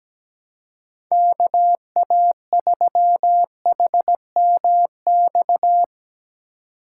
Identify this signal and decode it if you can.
Morse code, heard as a keyed tone, about 17 wpm, 700 Hz. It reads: KA3HMX
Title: USA Call Signs - 6 Characters - 4x Character Spacing - 17wpm